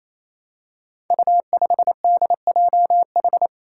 Morse code, heard as a keyed tone, about 28 wpm, 700 Hz. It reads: U5DJH
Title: Five Challenging Letter-Number Combinations 28wpm